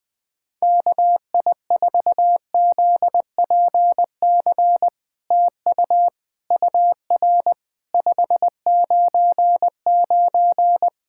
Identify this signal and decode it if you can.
Morse code, heard as a keyed tone, about 21 wpm, 700 Hz. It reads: KI4ZPC TU UR 599
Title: SOTA QSO Practice 20wpm